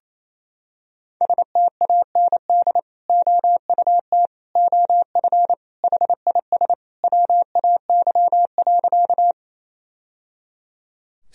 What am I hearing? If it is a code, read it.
Morse code, heard as a keyed tone, about 28 wpm, 700 Hz. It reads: STAND OUT OF HIS WAY.
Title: Sentences from Top 1000 Words - Review Part 1 - 28wpm